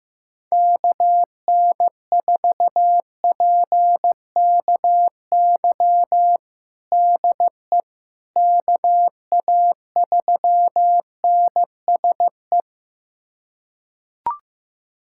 Morse code, heard as a keyed tone, about 16 wpm, 700 Hz. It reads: KN4PKY DE KA3NSE
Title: USA Callsign DE Callsign - 6 Characters - 15wpm